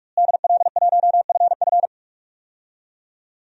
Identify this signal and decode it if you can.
Morse code, heard as a keyed tone, about 41 wpm, 700 Hz. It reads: DL1FF
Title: DXCC-100 Call Signs - 5 Characters - Repeated 2x - 45wpm